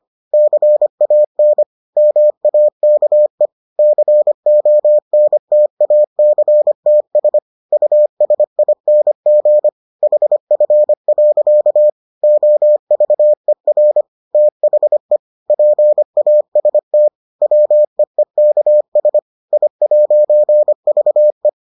Morse code, heard as a keyed tone, about 25 wpm, 600 Hz. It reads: CAN MAKE CONTACTS USING HF. OVER THE PAST WEEKS I'VE